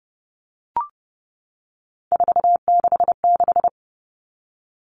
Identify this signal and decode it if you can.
Morse code, heard as a keyed tone, about 30 wpm, 700 Hz. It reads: 466
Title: All Cut Numbers - 3-Digit 30wpm